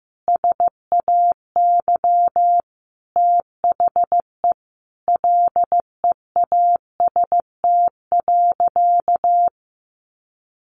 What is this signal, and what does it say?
Morse code, heard as a keyed tone, about 15 wpm, 700 Hz. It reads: SAY THE LEAST.
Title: Sentences from Top 1200 Words 15wpm